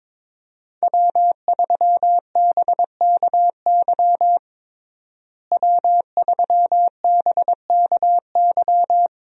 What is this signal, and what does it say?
Morse code, heard as a keyed tone, about 22 wpm, 700 Hz. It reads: W3BKY W3BKY
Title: USA Call Signs - 5 Characters - Repeated 2x - 22wpm